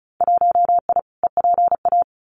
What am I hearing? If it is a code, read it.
Morse code, heard as a keyed tone, about 36 wpm, 700 Hz. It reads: 1I EPA
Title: Winter Field Day Exchanges Encore 35wpm